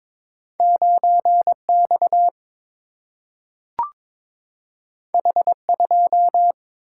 Morse code, heard as a keyed tone, about 22 wpm, 700 Hz. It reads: 9 X H 2
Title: Single-Letter Single-Number 22wpm